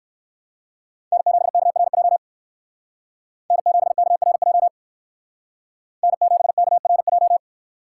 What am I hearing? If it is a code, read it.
Morse code, heard as a keyed tone, about 58 wpm, 700 Hz. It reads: N7KRJ N7KRJ N7KRJ